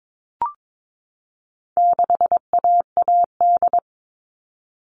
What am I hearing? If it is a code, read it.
Morse code, heard as a keyed tone, about 22 wpm, 700 Hz. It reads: 6AAD
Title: All Cut Numbers - 4-Digit 22wpm